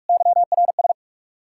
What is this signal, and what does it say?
Morse code, heard as a keyed tone, about 58 wpm, 700 Hz. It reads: YRS